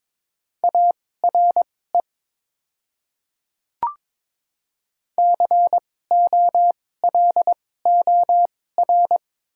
Morse code, heard as a keyed tone, about 22 wpm, spaced as for 16 wpm, 700 Hz. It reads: ARE COLOR
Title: 2x Character Spacing - Top 300 Words - 22wpm